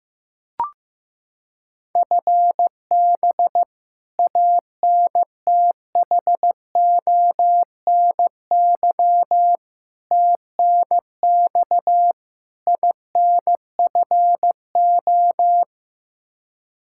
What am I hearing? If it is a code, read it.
Morse code, heard as a keyed tone, about 16 wpm, 700 Hz. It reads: FB ANTHONY TNX INFO
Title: K1USN SST QSO Practice 15wpm